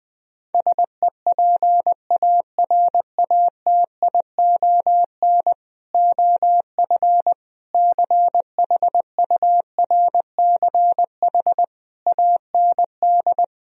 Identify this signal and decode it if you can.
Morse code, heard as a keyed tone, about 20 wpm, 700 Hz. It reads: SEPARATION OF CHURCH AND
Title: Speed-Racing - Sets of 5 Words 17wpm